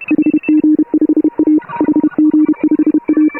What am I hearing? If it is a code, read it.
Morse code, heard as a keyed tone, about 31 wpm, 310 Hz. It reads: HG5A HG5A
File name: AUDIO2018 / HA3FMR CW